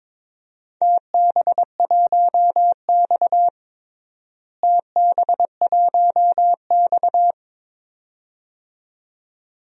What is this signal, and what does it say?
Morse code, heard as a keyed tone, about 22 wpm, 700 Hz. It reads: TB1X TB1X
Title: DXCC-100 Call Signs - 4 Characters - Repeated 2x - 22wpm